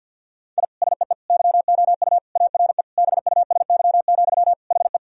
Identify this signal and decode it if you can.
Morse code, heard as a keyed tone, about 50 wpm, 700 Hz. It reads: I SEE YOU ARE BUSY, HE
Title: Sentences from Top 1000 Words - Review Part 3 - 50wpm